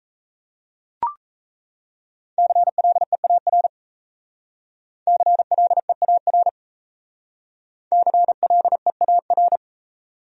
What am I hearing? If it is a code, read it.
Morse code, heard as a keyed tone, about 36 wpm, 700 Hz. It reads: CLEAR CLEAR CLEAR